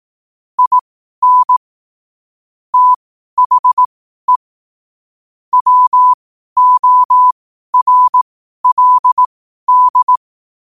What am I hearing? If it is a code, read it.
Morse code, heard as a keyed tone, about 18 wpm, spaced as for 12 wpm, 1000 Hz. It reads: IN THE WORLD
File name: qotdfiles / 12wpm